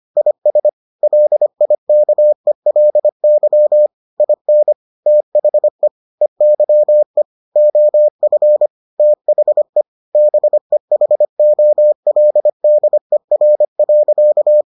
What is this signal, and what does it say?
Morse code, heard as a keyed tone, about 25 wpm, 600 Hz. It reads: IS LIKELY IN THE EYE OF THE BEHOLDER.